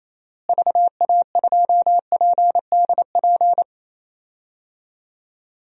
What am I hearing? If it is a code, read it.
Morse code, heard as a keyed tone, about 27 wpm, 700 Hz. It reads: VA2PDP
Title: DXCC-100 Call Signs - 6 Characters - Repeated 2x - 28wpm